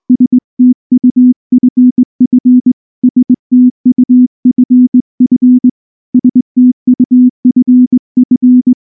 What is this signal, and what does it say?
Morse code, heard as a keyed tone, about 19 wpm, 250 Hz. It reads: STUFF STUFF STUFF
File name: morse / morse-training-20WPM-7